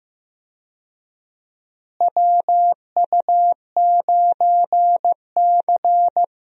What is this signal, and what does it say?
Morse code, heard as a keyed tone, about 15 wpm, 700 Hz. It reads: WU9C
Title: USA Call Signs - 4 Characters - Repeated 2x - 15wpm